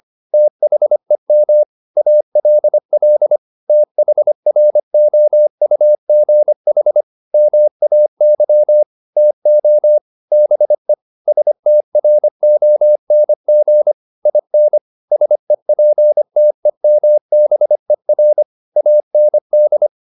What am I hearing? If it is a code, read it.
Morse code, heard as a keyed tone, about 25 wpm, 600 Hz. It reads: THEM ALL THROUGH MAY TO BE STRONG IN SEPTEMBER AND
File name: the old man and the sea part 1 25wpm